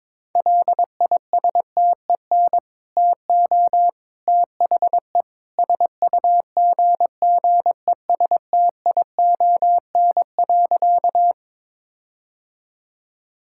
Morse code, heard as a keyed tone, about 22 wpm, 700 Hz. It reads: LISTEN TO THE SUGGESTION.
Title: Sentences from Top 1600 Words 22wpm